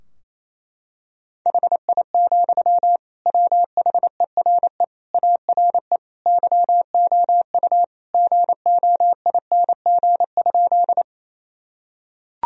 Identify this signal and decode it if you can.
Morse code, heard as a keyed tone, about 28 wpm, 700 Hz. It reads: HI, WHERE ARE YOU GOING?
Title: Sentences from Top 1000 Words - Review Part 3 - 28wpm